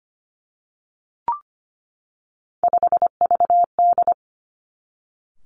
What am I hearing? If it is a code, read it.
Morse code, heard as a keyed tone, about 25 wpm, 700 Hz. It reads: 5VD
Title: Three Challenging Letter-Number Combinations 25wpm